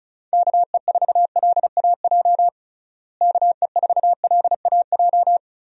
Morse code, heard as a keyed tone, about 36 wpm, 700 Hz. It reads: KE4LAJ KE4LAJ
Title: USA Call Signs - 6 Characters - Repeated 3x - 35wpm